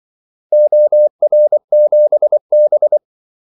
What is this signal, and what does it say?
Morse code, heard as a keyed tone, about 24 wpm, 600 Hz. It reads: OR7B
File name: Callsigns-24wpm